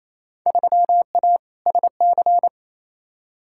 Morse code, heard as a keyed tone, about 28 wpm, 700 Hz. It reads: 3A SC
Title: Field Day Exchanges Encore - 28wpm